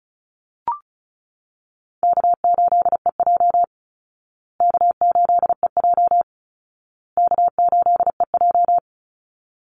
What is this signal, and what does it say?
Morse code, heard as a keyed tone, about 34 wpm, 700 Hz. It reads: K8EJ K8EJ K8EJ
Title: Call Signs Repeated 2x - 35wpm